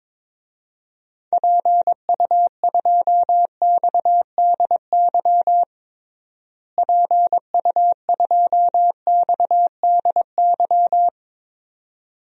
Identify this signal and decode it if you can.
Morse code, heard as a keyed tone, about 22 wpm, 700 Hz. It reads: PU2XDY PU2XDY
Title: DXCC-100 Call Signs - 6 Characters - Repeated 2x - 22wpm